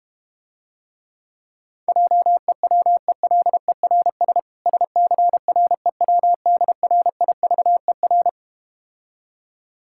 Morse code, heard as a keyed tone, about 32 wpm, 690 Hz. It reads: JEWELERS SCREWDRIVER